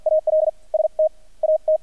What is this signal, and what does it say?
Morse code, heard as a keyed tone, about 48 wpm, spaced as for 32 wpm, 630 Hz, with radio noise. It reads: AL STAT
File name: archived / CRSnet